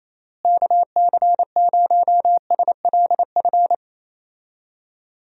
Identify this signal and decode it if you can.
Morse code, heard as a keyed tone, about 28 wpm, 700 Hz. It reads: KC0SLF
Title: USA Call Signs - 6 Characters - Repeated 3x - 28wpm